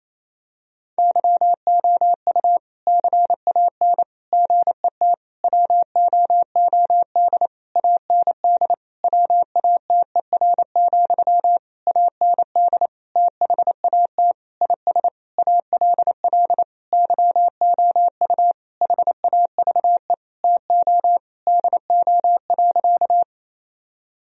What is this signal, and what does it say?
Morse code, heard as a keyed tone, about 28 wpm, 700 Hz. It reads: YOU CAN GET WOOD AND WATER, AND THAT IS ALL YOU HAVE TO DO.
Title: Sentences from Top 1000 Words - Review Part 2 - 28wpm